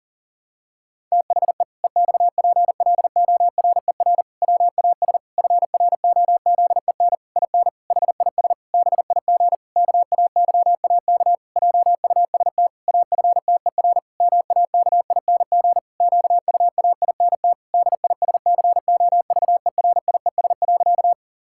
Morse code, heard as a keyed tone, about 40 wpm, 700 Hz. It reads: THE EXPLORER WAS FROZEN IN HIS BIG KAYAK JUST AFTER MAKING QUAINT DISCOVERIES.
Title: Pangram Challenge v2 - 40wpm